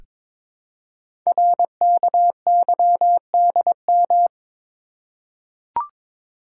Morse code, heard as a keyed tone, about 22 wpm, 700 Hz. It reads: RKYDM
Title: Code Groups - 5 Letters - 22wpm